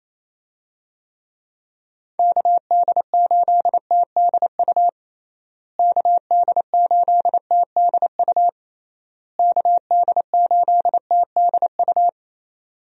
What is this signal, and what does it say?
Morse code, heard as a keyed tone, about 28 wpm, 700 Hz. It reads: KD8TDU KD8TDU KD8TDU
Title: USA Call Signs - 6 Characters - Repeated 3x - 28wpm